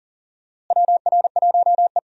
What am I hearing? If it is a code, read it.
Morse code, heard as a keyed tone, about 40 wpm, 700 Hz. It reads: WR1E
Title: USA Call Signs - 4 Characters - Repeated 3x - 40wpm